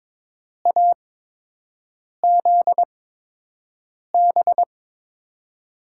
Morse code, heard as a keyed tone, about 22 wpm, spaced as for 6 wpm, 700 Hz. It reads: AZB
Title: USA Call Signs - 6 Characters - 8x Character Spacing - 22wpm